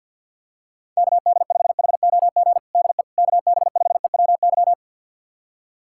Morse code, heard as a keyed tone, about 50 wpm, 690 Hz. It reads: KD5HOG DE KB5EWY
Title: USA Callsign DE Callsign - 6 Characters - 50wpm